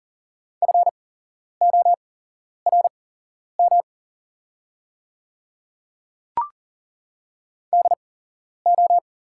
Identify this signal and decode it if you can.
Morse code, heard as a keyed tone, about 40 wpm, spaced as for 11 wpm, 700 Hz. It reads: FORM DO
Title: 8x Character Spacing - Top 200 Words - 40wpm